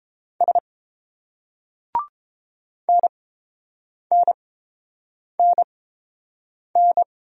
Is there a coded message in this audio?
Yes: Morse code from 0.4 to 7.0 s, heard as a keyed tone, about 30 wpm, spaced as for 7 wpm, 700 Hz. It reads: S NNNN